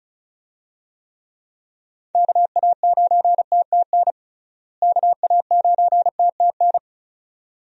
Morse code, heard as a keyed tone, about 36 wpm, 700 Hz. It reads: KA9TTN KA9TTN
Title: USA Call Signs - 6 Characters - Repeated 2x - 35wpm